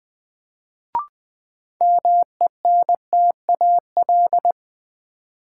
Morse code, heard as a keyed tone, about 20 wpm, 700 Hz. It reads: MENTAL
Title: Sentences from Top 1000 Words - Review Part 4 - New Word Drill - 20wpm